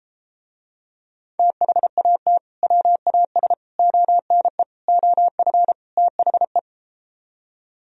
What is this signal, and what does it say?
Morse code, heard as a keyed tone, about 33 wpm, 700 Hz. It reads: THAT WAS ONE OF THE